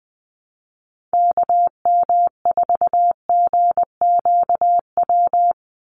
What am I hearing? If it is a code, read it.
Morse code, heard as a keyed tone, about 20 wpm, 700 Hz. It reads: KM4GQW
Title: USA Call Signs 20wpm